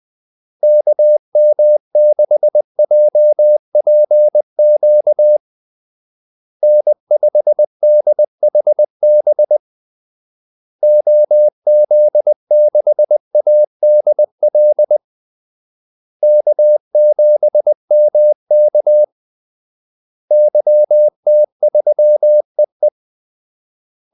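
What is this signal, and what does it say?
Morse code, heard as a keyed tone, about 20 wpm, 600 Hz. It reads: KM6JPQ N5DHB OZ6ADL K7MK YT3EE
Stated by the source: Morse call-sign practice at 20 wpm.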